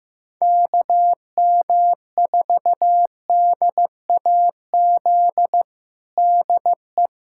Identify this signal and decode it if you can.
Morse code, heard as a keyed tone, about 15 wpm, 700 Hz. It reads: KM4DAZ DE
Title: USA Callsign DE Callsign 15wpm